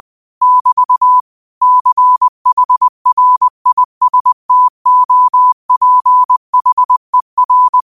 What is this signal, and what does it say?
Morse code, heard as a keyed tone, about 20 wpm, 1000 Hz. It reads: <BT> CHRISTOPHER
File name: qotdfiles / 20wpm